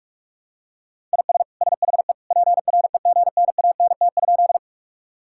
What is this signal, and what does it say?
Morse code, heard as a keyed tone, about 46 wpm, 700 Hz. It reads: IS SHE PREGNANT?
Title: Sentences from Top 1000 Words - Review Part 6 - 45wpm